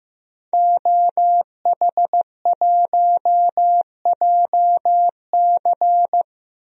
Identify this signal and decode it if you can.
Morse code, heard as a keyed tone, about 15 wpm, 700 Hz. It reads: OH1JC
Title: DXCC-100 Call Signs - 5 Characters - 15wpm